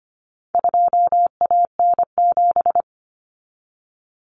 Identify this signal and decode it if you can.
Morse code, heard as a keyed tone, about 25 wpm, 700 Hz. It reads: 2AN7
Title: Common Cut Numbers - 4-Digit 25wpm